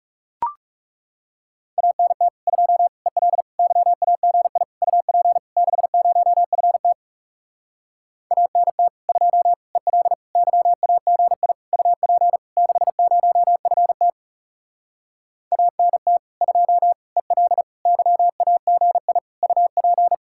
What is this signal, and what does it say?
Morse code, heard as a keyed tone, about 40 wpm, 700 Hz. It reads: ANT 2 EL YAGI UP 60FT ANT 2 EL YAGI UP 60FT ANT 2 EL YAGI UP